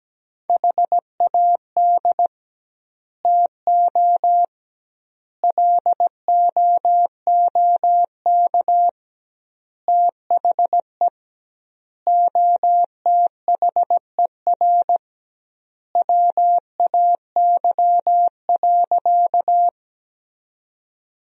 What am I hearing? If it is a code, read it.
Morse code, heard as a keyed tone, about 17 wpm, 700 Hz. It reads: HAD TO LOOK THE OTHER WAY.
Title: 2x Word-Spacing - Sentences from Top 100 Words - 17wpm